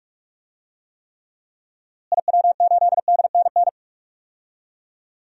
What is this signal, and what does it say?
Morse code, heard as a keyed tone, about 46 wpm, 700 Hz. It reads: IW8DNN
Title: DXCC-100 Call Signs - 6 Characters - Repeated 3x - 45wpm